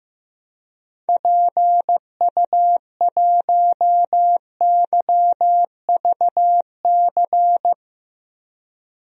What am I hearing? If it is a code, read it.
Morse code, heard as a keyed tone, about 15 wpm, 700 Hz. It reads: PU1YVC
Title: DXCC-100 Call Signs - 6 Characters - Repeated 3x - 15wpm